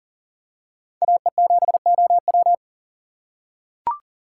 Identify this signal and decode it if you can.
Morse code, heard as a keyed tone, about 40 wpm, 700 Hz. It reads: AE7OW
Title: USA Call Signs - 5 Characters - 4x Character Spacing - 40wpm